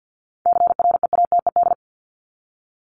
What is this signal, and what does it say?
Morse code, heard as a keyed tone, about 50 wpm, 700 Hz. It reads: CREATED
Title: ED Suffix - Top 100 Words with Base Words 50wpm